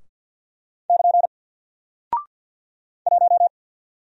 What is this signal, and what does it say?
Morse code, heard as a keyed tone, about 50 wpm, 700 Hz. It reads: <KN> 1
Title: Single Character (Letter:Number:Punctuation:Prosign) Alt 50wpm